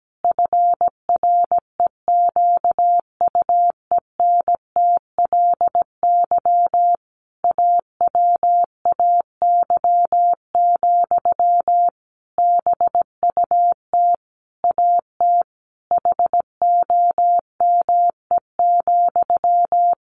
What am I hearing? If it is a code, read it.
Morse code, heard as a keyed tone, about 17 wpm, 700 Hz. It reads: FREQUENTLY AWAY, BUT AT HOME,
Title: Sentences from Top 1400 Words 17wpm